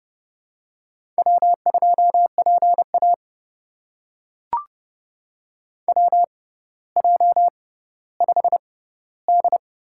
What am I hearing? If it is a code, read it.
Morse code, heard as a keyed tone, about 30 wpm, spaced as for 15 wpm, 700 Hz. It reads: W2PA WJ5D
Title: USA Call Signs - 4 Characters - 6x Character Spacing - 30wpm